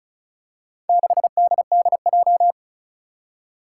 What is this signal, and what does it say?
Morse code, heard as a keyed tone, about 36 wpm, 700 Hz. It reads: 6DDJ
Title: Four Challenging Letter-Number Combinations 35wpm